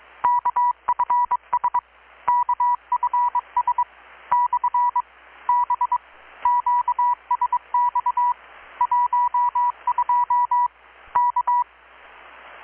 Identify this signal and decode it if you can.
Morse code, heard as a keyed tone, about 22 wpm, 980 Hz, with radio noise. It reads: KFS KFS / B QSX 12 K
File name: blog / swl / 6-abril-2019-12695-USB-KPS